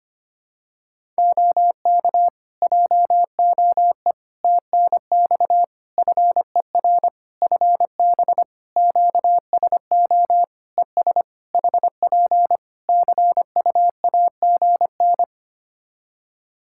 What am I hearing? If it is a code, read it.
Morse code, heard as a keyed tone, about 25 wpm, 700 Hz. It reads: OK JOE TNX FER FB QSO ES HP CUAGN
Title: Practice QSO - Exchange 4 - Ending - 25wpm